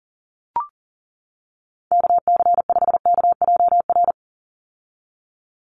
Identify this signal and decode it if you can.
Morse code, heard as a keyed tone, about 40 wpm, 700 Hz. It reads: KC5KJR
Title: USA Call Signs 40wpm